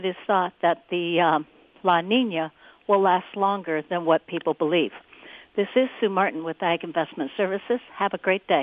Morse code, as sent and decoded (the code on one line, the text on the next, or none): none